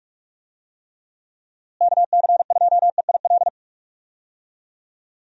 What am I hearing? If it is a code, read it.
Morse code, heard as a keyed tone, about 46 wpm, 700 Hz. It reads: KC2EIL
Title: USA Call Signs - 6 Characters - Repeated 2x - 45wpm